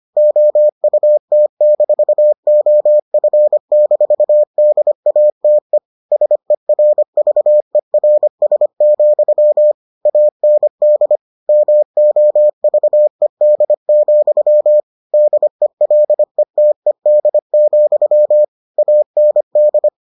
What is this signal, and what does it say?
Morse code, heard as a keyed tone, about 25 wpm, 600 Hz. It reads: OUT-OF-DATE SERVERS, AND MOVED, DELETED, AND